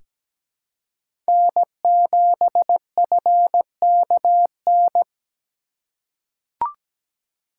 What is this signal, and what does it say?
Morse code, heard as a keyed tone, about 17 wpm, 700 Hz. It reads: N7FKN
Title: USA Call Signs - 5 Characters - 3x Character Spacing - 17wpm